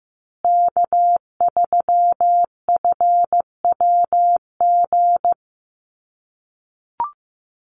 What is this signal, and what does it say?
Morse code, heard as a keyed tone, about 15 wpm, 700 Hz. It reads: K3FWG
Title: USA Call Signs Repeated 3x 15wpm